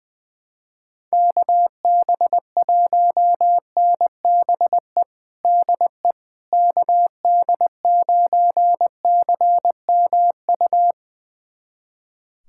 Morse code, heard as a keyed tone, about 20 wpm, 700 Hz. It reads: KB1NBE DE KD9CMU
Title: USA Callsign DE Callsign - 6 Characters - 20wpm